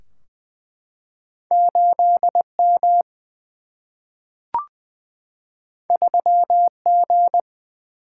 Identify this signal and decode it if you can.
Morse code, heard as a keyed tone, about 20 wpm, 700 Hz. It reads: 8 M 3 G
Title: Single-Letter Single-Number 20wpm